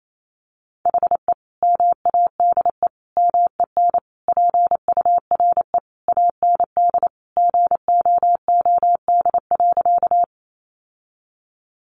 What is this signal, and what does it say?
Morse code, heard as a keyed tone, about 28 wpm, 700 Hz. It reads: HE MADE MEN PURE AND GOOD.
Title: Sentences from Top 1500 Words 28wpm